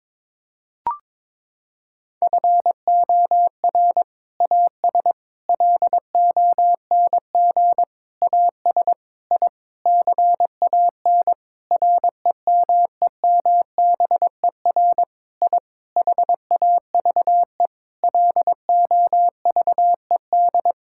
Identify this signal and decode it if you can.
Morse code, heard as a keyed tone, about 22 wpm, 700 Hz. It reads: FOR AS LONG AS I CAN REMEMBER I HAVE LOVED
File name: practice / 22wpm / Sentences with 8-Word Phrases - 22wpm